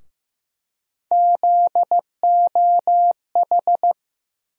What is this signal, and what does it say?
Morse code, heard as a keyed tone, about 15 wpm, 700 Hz. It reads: ZOH
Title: USA Call Sign Postfixes - 3 Letters 15wpm